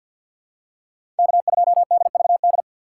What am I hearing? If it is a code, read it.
Morse code, heard as a keyed tone, about 49 wpm, 700 Hz. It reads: K2DVD
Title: USA Call Signs - 5 Characters - 6x Character Spacing - 50wpm